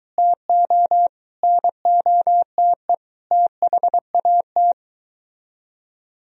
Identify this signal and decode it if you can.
Morse code, heard as a keyed tone, about 23 wpm, 700 Hz. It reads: TO NOTE THAT